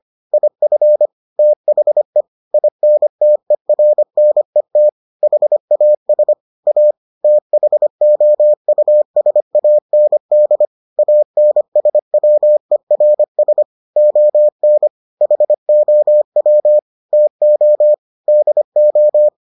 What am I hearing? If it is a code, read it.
Morse code, heard as a keyed tone, about 25 wpm, 600 Hz. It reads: IF THE INTERNET HAS A THOUSAND ANSWERS ON HOW TO DO